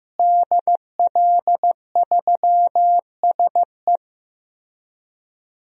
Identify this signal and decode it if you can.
Morse code, heard as a keyed tone, about 15 wpm, 700 Hz. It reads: DL3SE
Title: DXCC-100 Call Signs - 5 Characters - Repeated 2x - 15wpm